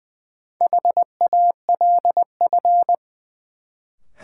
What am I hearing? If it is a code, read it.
Morse code, heard as a keyed tone, about 20 wpm, 700 Hz. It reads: HALF
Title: Sets of 3 Words - New Word Drill - 20wpm